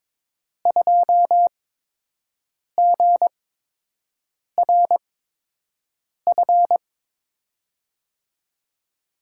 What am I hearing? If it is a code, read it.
Morse code, heard as a keyed tone, about 23 wpm, spaced as for 6 wpm, 700 Hz. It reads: 2GRF
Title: USA Call Signs - 8x Character Spacing - 22wpm